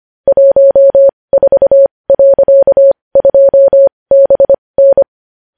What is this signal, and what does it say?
Morse code, heard as a keyed tone, about 25 wpm, spaced as for 20 wpm, 550 Hz. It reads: 14.2BN
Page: news-headlines-in-morse-code-at-20-wpm